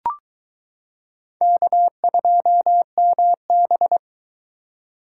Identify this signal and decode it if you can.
Morse code, heard as a keyed tone, about 22 wpm, 700 Hz. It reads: K2MB